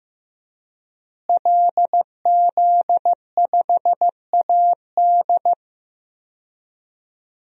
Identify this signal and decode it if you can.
Morse code, heard as a keyed tone, about 15 wpm, 700 Hz. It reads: LZ5AD
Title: DXCC-100 Call Signs - 5 Characters - Repeated 2x - 15wpm